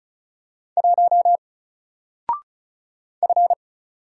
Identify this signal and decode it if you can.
Morse code, heard as a keyed tone, about 35 wpm, 700 Hz. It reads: 1 F
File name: Single-Letter-Number-35wpm